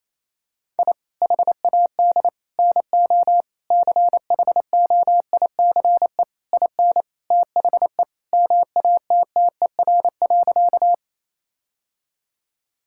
Morse code, heard as a keyed tone, about 28 wpm, 700 Hz. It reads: I HAD NO CHOICE IN THE MATTER.
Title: Sentences from Top 1000 Words - Review Part 6 - 28wpm